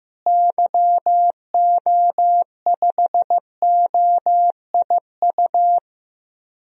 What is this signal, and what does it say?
Morse code, heard as a keyed tone, about 15 wpm, 700 Hz. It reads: YO5OIU
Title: DXCC-100 Call Signs - 6 Characters - Repeated 3x - 15wpm